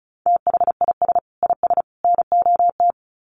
Morse code, heard as a keyed tone, about 34 wpm, 700 Hz. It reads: THIS IS NOT
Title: Sets of 3 Words 35wpm